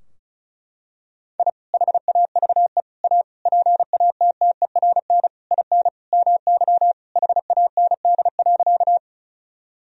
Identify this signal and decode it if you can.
Morse code, heard as a keyed tone, about 34 wpm, 700 Hz. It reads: I HAVE A PATTERN IN MY HAND.
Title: Sentences from Top 1000 Words - Review Part 2 - 35wpm